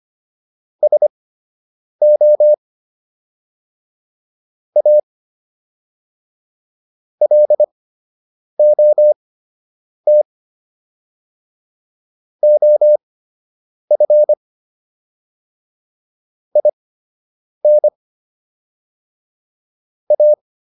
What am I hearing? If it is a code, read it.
Morse code, heard as a keyed tone, about 25 wpm, spaced as for 8 wpm, 600 Hz. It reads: SO A LOT OF IN A